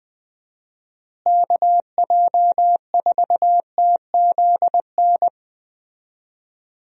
Morse code, heard as a keyed tone, about 21 wpm, 700 Hz. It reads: KJ4TZN
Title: USA Call Signs - 6 Characters - Repeated 2x - 20wpm